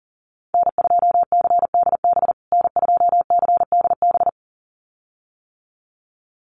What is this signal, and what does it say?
Morse code, heard as a keyed tone, about 40 wpm, 700 Hz. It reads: N2CDB N2CDB
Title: USA Call Signs Repeated 2x 40wpm